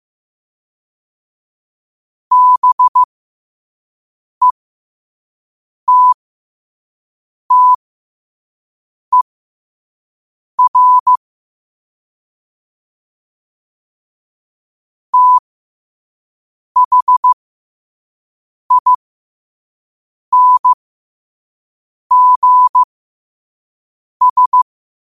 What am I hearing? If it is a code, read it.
Morse code, heard as a keyed tone, about 15 wpm, spaced as for 5 wpm, 1000 Hz. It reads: BETTER THINGS
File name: qotdfiles / 5wpm